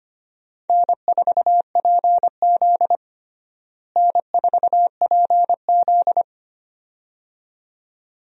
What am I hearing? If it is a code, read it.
Morse code, heard as a keyed tone, about 25 wpm, 700 Hz. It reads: N4PZ N4PZ
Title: USA Call Signs - 4 Characters - Repeated 2x - 25wpm